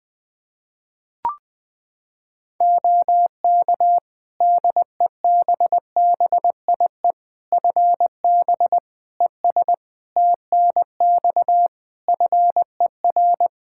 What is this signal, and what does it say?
Morse code, heard as a keyed tone, about 21 wpm, 700 Hz. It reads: OK DEBBIE FB ES TNX FER
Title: Practice QSO - Exchange 2 - Info - 20wpm